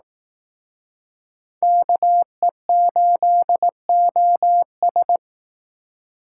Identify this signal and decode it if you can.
Morse code, heard as a keyed tone, about 18 wpm, 700 Hz. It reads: KE8OS
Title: Speed Racing - USA 2x2 Call Signs 15wpm